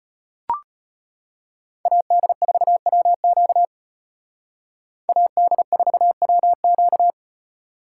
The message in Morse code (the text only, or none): AD4WQ AD4WQ